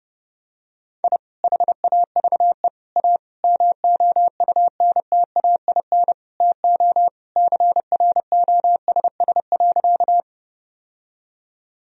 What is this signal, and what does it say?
Morse code, heard as a keyed tone, about 30 wpm, 700 Hz. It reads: I HAVE A MOUNTAIN TO CROSS.
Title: Sentences from Top 1000 Words - Review Part 1 - 30wpm